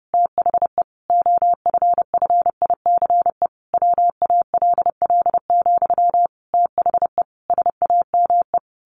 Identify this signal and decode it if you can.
Morse code, heard as a keyed tone, about 30 wpm, 700 Hz. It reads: THE OFFICE WALL, THE SAME
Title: A Sound Of Thunder 30wpm